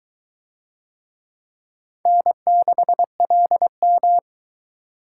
Morse code, heard as a keyed tone, about 20 wpm, 700 Hz. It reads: N6LM